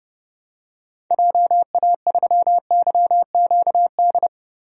Morse code, heard as a keyed tone, about 30 wpm, 700 Hz. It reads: JA3YQD
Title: DXCC-100 Call Signs - 6 Characters - Repeated 3x - 30wpm